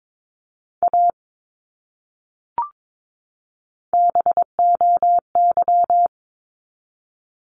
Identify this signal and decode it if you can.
Morse code, heard as a keyed tone, about 22 wpm, 700 Hz. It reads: A BOY
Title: Top 200 Words 22wpm